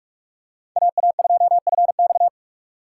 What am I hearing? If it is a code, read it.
Morse code, heard as a keyed tone, about 46 wpm, 700 Hz. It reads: AA2FX
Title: USA Call Signs - 5 Characters - 6x Character Spacing - 45wpm